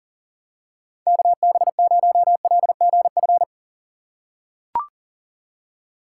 Morse code, heard as a keyed tone, about 40 wpm, 700 Hz. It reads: KB0LGF
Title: K1USN SST QSO Practice - 40wpm